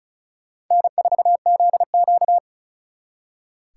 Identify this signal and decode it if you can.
Morse code, heard as a keyed tone, about 44 wpm, 700 Hz. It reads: N4ZQ